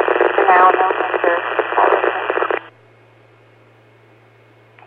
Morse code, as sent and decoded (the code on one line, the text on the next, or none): none